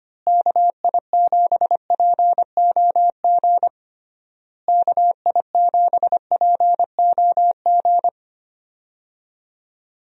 Morse code, heard as a keyed tone, about 25 wpm, 700 Hz. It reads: KI7POG KI7POG
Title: USA Call Signs - 6 Characters - Repeated 3x - 25wpm